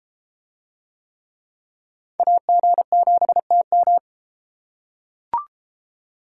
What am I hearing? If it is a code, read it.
Morse code, heard as a keyed tone, about 33 wpm, 700 Hz. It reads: AG7TM